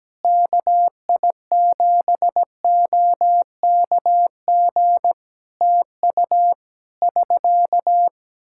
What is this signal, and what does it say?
Morse code, heard as a keyed tone, about 17 wpm, 700 Hz. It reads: KI7OKG TU <SK>
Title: K1USN SST QSO Practice 17wpm